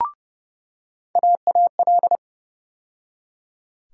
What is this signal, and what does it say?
Morse code, heard as a keyed tone, about 32 wpm, 700 Hz. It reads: AAL
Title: USA Call Sign Postfixes - 3 Letters 30wpm